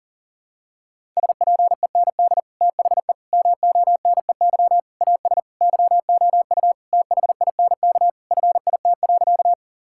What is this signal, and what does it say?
Morse code, heard as a keyed tone, about 37 wpm, 700 Hz. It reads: SPEND THE MONEY AS YOU THINK FIT.
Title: Sentences from Top 1000 Words - Review Part 5 - 40wpm